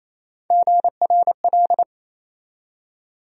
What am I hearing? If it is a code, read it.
Morse code, heard as a keyed tone, about 28 wpm, 700 Hz. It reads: GRL
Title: USA Call Sign Postfixes - 3 Letters 28wpm